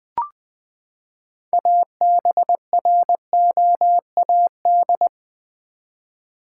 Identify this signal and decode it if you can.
Morse code, heard as a keyed tone, about 21 wpm, 700 Hz. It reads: ABROAD
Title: Six Letter Words - 20wpm